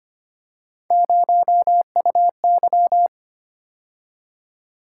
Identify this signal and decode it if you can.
Morse code, heard as a keyed tone, about 25 wpm, 700 Hz. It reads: 0UY
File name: Three Letter-Number Combinations - 25wpm